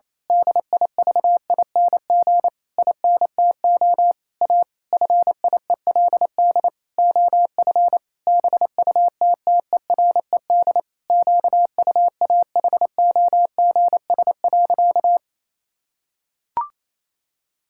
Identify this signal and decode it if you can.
Morse code, heard as a keyed tone, about 27 wpm, 700 Hz. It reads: DIVING INTO A FIELD OF BUTTERED QUAHOGS.
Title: Pangram Challenge v2 - 28wpm